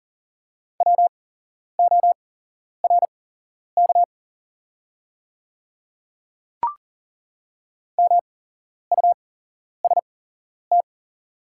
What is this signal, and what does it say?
Morse code, heard as a keyed tone, about 40 wpm, spaced as for 11 wpm, 700 Hz. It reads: WORK MUST
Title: 8x Character Spacing - Top 200 Words - 40wpm